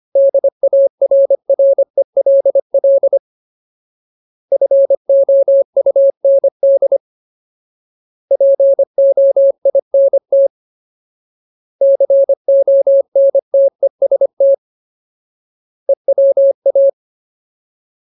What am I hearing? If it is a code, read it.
Morse code, heard as a keyed tone, about 25 wpm, 550 Hz. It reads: DARRELL FOUND POINT CONTEST EWA
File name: Ultimate Word List / 25WPM Ultimate